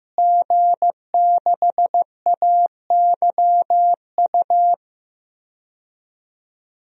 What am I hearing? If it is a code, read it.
Morse code, heard as a keyed tone, about 15 wpm, 700 Hz. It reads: G6AYU
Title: DXCC-100 Call Signs - 5 Characters - Repeated 3x - 15wpm